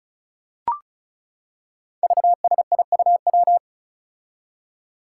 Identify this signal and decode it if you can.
Morse code, heard as a keyed tone, about 36 wpm, 700 Hz. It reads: VSIUW
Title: Five Challenging Letter-Number Combinations 35wpm